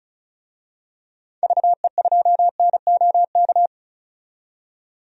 Morse code, heard as a keyed tone, about 34 wpm, 700 Hz. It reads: VE2NOK
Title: DXCC-100 Call Signs - 6 Characters - 35wpm